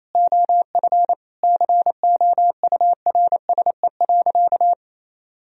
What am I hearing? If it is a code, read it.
Morse code, heard as a keyed tone, about 28 wpm, 700 Hz. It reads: OF COURSE.
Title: Sentences from Top 1000 Words - Review Part 3 - 28wpm